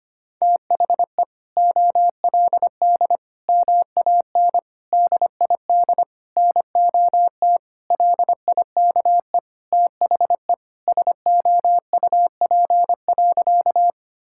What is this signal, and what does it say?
Morse code, heard as a keyed tone, about 25 wpm, 700 Hz. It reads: THE OLD MAN DID NOT LIKE THE SOUP.
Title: Sentences from Top 1000 Words - Review Part 6 - 25wpm